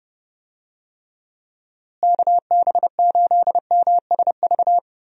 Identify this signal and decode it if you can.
Morse code, heard as a keyed tone, about 30 wpm, 700 Hz. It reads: KB8MSV
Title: USA Call Signs - 6 Characters - Repeated 2x - 30wpm